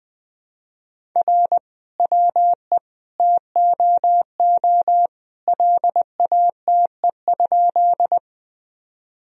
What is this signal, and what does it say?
Morse code, heard as a keyed tone, about 21 wpm, 700 Hz. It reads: R WE TOO LATE?
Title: Sentences from Top 300 Words and Abbreviations 20wpm